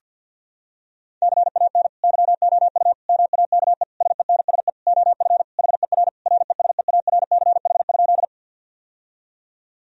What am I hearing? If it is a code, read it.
Morse code, heard as a keyed tone, about 50 wpm, 700 Hz. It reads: CAN YOU MAKE SENSE OF HER RESEARCH?
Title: Sentences from Top 1000 Words - Review Part 4 - 50wpm